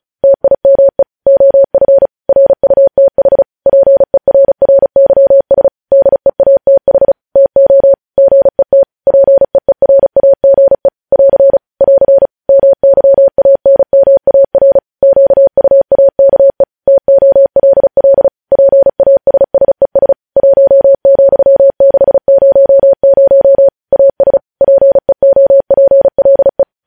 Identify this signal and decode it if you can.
Morse code, heard as a keyed tone, about 36 wpm, 550 Hz. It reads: TIME OF RUTH PERRYS DEATH TO GET PEERAGE <AR> <AR> MYANMAR QUAKE TOLL PASSES 1,600 AS PEOPLE
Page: News headlines in Morse code at 35 wpm